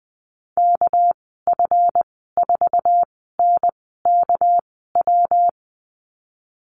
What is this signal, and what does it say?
Morse code, heard as a keyed tone, about 20 wpm, spaced as for 14 wpm, 700 Hz. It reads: KF4NKW
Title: 238 Lesson 38g - Letter q - Call Signs 20wpm